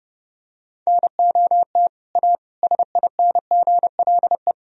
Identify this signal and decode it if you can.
Morse code, heard as a keyed tone, about 32 wpm, 700 Hz. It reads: NOT A SINGLE
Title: Sentences from Top 1000 Words - Review Part 4 - 30wpm